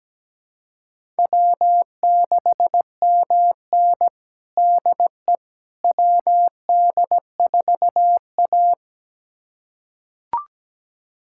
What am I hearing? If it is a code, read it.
Morse code, heard as a keyed tone, about 17 wpm, 700 Hz. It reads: W6MN DE WD4A
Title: USA Callsign DE Callsign - 4 Characters - 17wpm